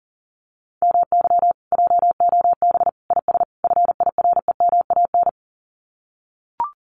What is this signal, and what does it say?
Morse code, heard as a keyed tone, about 40 wpm, 700 Hz. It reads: MY JOB IS FIREMAN
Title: Common QSO Phrases Encore 40wpm